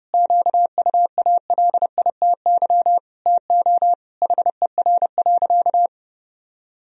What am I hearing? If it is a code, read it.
Morse code, heard as a keyed tone, about 30 wpm, 700 Hz. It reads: QUALITY TO HER.
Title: Sentences from Top 1000 Words - Review Part 4 - 30wpm